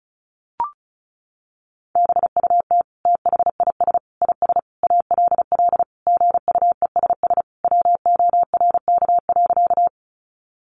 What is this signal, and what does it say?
Morse code, heard as a keyed tone, about 34 wpm, 700 Hz. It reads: BUT THIS IS ALL GUESS WORK.
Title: Sentences from Top 1500 Words 35wpm